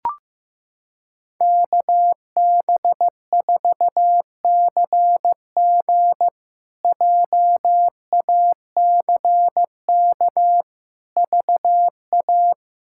Morse code, heard as a keyed tone, about 15 wpm, 700 Hz. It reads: KB4CG JACK VA
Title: K1USN SST QSO Practice - 15wpm